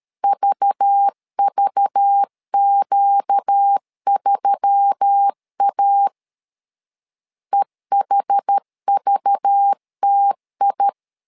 Morse code, heard as a keyed tone, about 13 wpm, 800 Hz. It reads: VVQ3A EHVTI
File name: LEZIONI / 5 LBVC / 5 LBV 12wpm 1855